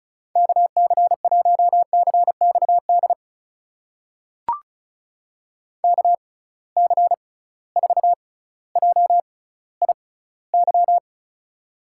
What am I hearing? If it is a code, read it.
Morse code, heard as a keyed tone, about 36 wpm, spaced as for 18 wpm, 700 Hz. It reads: KC1CXD KC4JIY
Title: USA Call Signs - 6 Characters - 6x Character Spacing - 35wpm